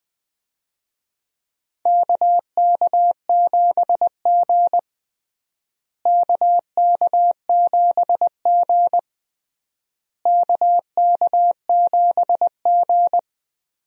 Morse code, heard as a keyed tone, about 20 wpm, 700 Hz. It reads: KK7G KK7G KK7G
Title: USA Call Signs - 4 Characters - Repeated 3x - 20wpm